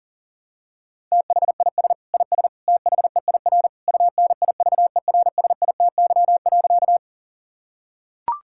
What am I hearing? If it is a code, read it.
Morse code, heard as a keyed tone, about 40 wpm, 700 Hz. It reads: THIS IS THEIR UNIVERSITY.
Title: Sentences from Top 1000 Words - Review Part 5 - 40wpm